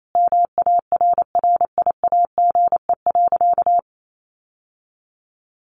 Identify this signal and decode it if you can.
Morse code, heard as a keyed tone, about 28 wpm, 700 Hz. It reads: MARRIAGE.
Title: Sentences from Top 1400 Words 28wpm